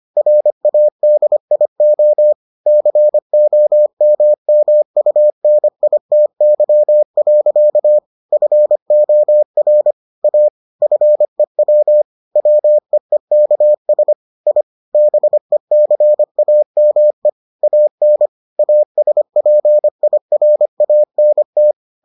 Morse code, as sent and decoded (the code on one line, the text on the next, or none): .-. .- -.. .. --- / -.-. --- -- -- ..- -. .. - -.-- .-.-.- / ..-. --- .-. / .- / ..-. .-- / .-- . -.- ... / .. / -... -.-. .- -- . / .- -. / .- ... .--. .. .-. .- -. -
RADIO COMMUNITY. FOR A FEW WEEKS I BECAME AN ASPIRANT